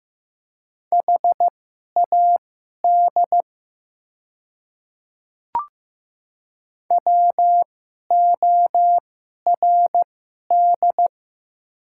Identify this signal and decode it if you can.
Morse code, heard as a keyed tone, about 15 wpm, spaced as for 11 wpm, 700 Hz. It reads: HAD WORD
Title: 2x Character Spacing - Top 100 Words - 15wpm